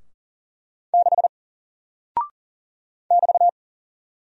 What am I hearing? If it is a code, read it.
Morse code, heard as a keyed tone, about 40 wpm, 700 Hz. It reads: 6 <BT>
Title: Single Character (Letter:Number:Punctuation:Prosign) Alt 40wpm